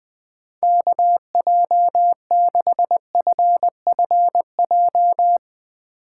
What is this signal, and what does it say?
Morse code, heard as a keyed tone, about 20 wpm, 700 Hz. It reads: KJ6FFJ
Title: Speed Racing - USA 2x3 Call Signs 20wpm